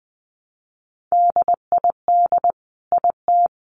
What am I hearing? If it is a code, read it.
Morse code, heard as a keyed tone, about 20 wpm, 700 Hz. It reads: DID IT
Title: Sentences from Top 1200 Words 20wpm